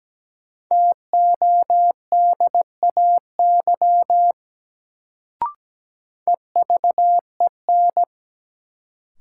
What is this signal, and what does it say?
Morse code, heard as a keyed tone, about 17 wpm, 700 Hz. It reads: TODAY EVEN
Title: Sentences from Top 600 Words - New Word Drill - 17wpm